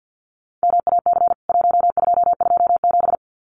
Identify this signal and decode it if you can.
Morse code, heard as a keyed tone, about 48 wpm, 700 Hz. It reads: MAC 1227
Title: CWT Exchange 50wpm